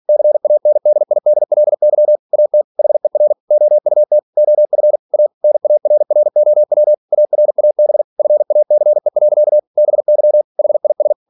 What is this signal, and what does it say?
Morse code, heard as a keyed tone, about 46 wpm, 600 Hz. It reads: CANDIDLY AT HER OUT OF A NARROW ARAB FACE. BY HIS
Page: Never 47wpm